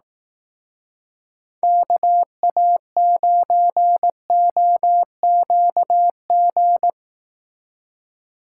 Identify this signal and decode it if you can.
Morse code, heard as a keyed tone, about 18 wpm, 700 Hz. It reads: KA9OQG
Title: Speed Racing - USA 2x3 Call Signs 15wpm